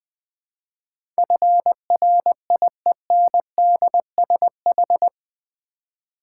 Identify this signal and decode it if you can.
Morse code, heard as a keyed tone, about 20 wpm, 700 Hz. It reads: FRIENDSH
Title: Build A Word - 4 Letters At A Time - 10-Letter Words - 20wpm